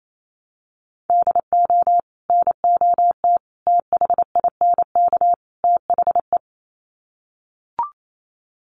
Morse code, heard as a keyed tone, about 28 wpm, 700 Hz. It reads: DO NOT THINK THE
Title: Sets of 4 Words 28wpm